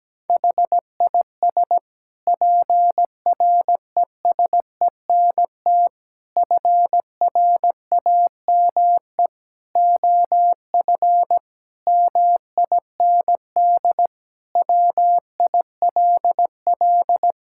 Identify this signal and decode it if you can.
Morse code, heard as a keyed tone, about 17 wpm, 700 Hz. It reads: HIS PRESENT FRAME OF MIND WILL
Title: Sentences from Top 1000 Words - Review Part 6 - 17wpm